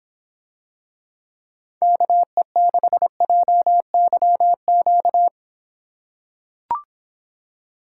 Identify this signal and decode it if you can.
Morse code, heard as a keyed tone, about 25 wpm, 700 Hz. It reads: KE6JYQ